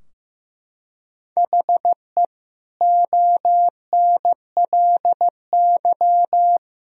Morse code, heard as a keyed tone, about 15 wpm, 700 Hz. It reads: HE ONLY
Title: Sentences from Top 1000 Words - Review Part 2 - 15wpm